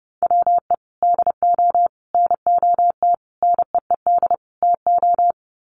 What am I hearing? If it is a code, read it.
Morse code, heard as a keyed tone, about 30 wpm, 700 Hz. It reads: WE DO NOT NEED TO
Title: Sets of 5 Words Encore 30wpm